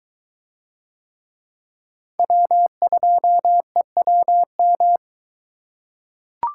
Morse code, heard as a keyed tone, about 21 wpm, 700 Hz. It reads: W2EWM